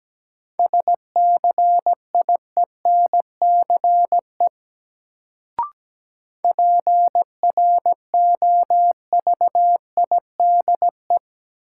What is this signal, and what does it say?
Morse code, heard as a keyed tone, about 17 wpm, 700 Hz. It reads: SCIENCE PROVIDE
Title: Sentences from Top 1000 Words - Review Part 2 - New Word Drill - 17wpm